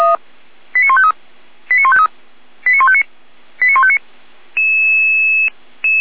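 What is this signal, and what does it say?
Five-tone selective call (ZVEI), heard as quick runs of five tones, tones of about 70 ms. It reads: 89153 89153 89159 89159
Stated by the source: ZVEI five-tone selective call 89150 received by SDR.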